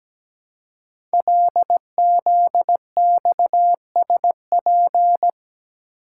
Morse code, heard as a keyed tone, about 17 wpm, 700 Hz. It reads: LZXSP
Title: Code Groups - 5 Letters - 17wpm